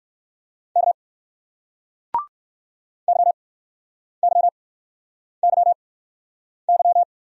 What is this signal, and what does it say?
Morse code, heard as a keyed tone, about 62 wpm, spaced as for 9 wpm, 700 Hz. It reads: K YYYY